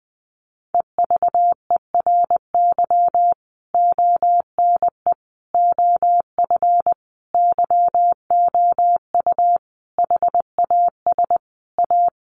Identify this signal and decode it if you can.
Morse code, heard as a keyed tone, about 20 wpm, 700 Hz. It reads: EVERY ONE OF YOU HAS A
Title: Sentences from Top 1500 Words 20wpm